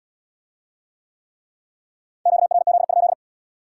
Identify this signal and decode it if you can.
Morse code, heard as a keyed tone, about 77 wpm, 700 Hz. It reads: QRZ?